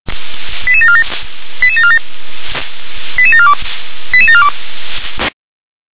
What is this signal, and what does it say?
Five-tone selective call (ZVEI), heard as quick runs of five tones, tones of about 70 ms. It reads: 88747 88747 88742 88742